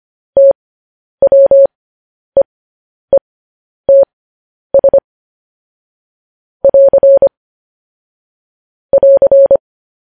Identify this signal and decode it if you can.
Morse code, heard as a keyed tone, about 25 wpm, spaced as for 10 wpm, 550 Hz. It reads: TWEETS <AR> <AR>
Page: news headlines in morse code at 10 wpm